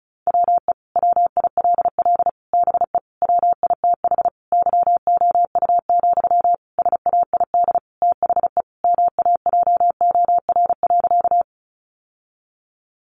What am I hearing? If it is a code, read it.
Morse code, heard as a keyed tone, about 34 wpm, 700 Hz. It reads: WE WILL BE WITH YOU, SAID THE MAJOR.
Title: Sentences from Top 600 Words 35wpm